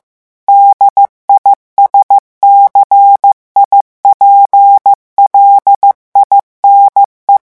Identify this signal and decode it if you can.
Morse code, heard as a keyed tone, about 15 wpm, 800 Hz. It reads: DISCIPLINE
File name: morse mp3 / revision9 citations humoristiques vitesse 14 mots minute